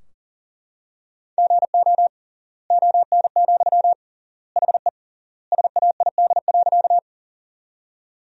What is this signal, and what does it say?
Morse code, heard as a keyed tone, about 40 wpm, 700 Hz. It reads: GO ON, HE SAID.
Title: 3x Word-Spacing - Sentences from Top 100 Words - 40wpm